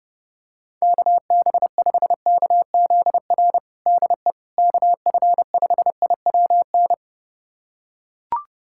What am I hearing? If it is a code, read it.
Morse code, heard as a keyed tone, about 30 wpm, 700 Hz. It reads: KB5KZR DE KF5IWN
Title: USA Callsign DE Callsign - 6 Characters - 30wpm